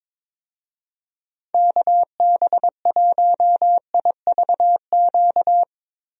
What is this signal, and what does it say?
Morse code, heard as a keyed tone, about 22 wpm, 700 Hz. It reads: KB1IVQ
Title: USA Call Signs - 6 Characters - Repeated 3x - 22wpm